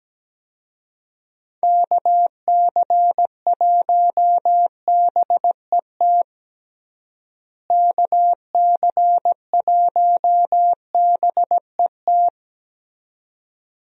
Morse code, heard as a keyed tone, about 17 wpm, 700 Hz. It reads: KC1BET KC1BET
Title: USA Call Signs - 6 Characters - Repeated 2x - 17wpm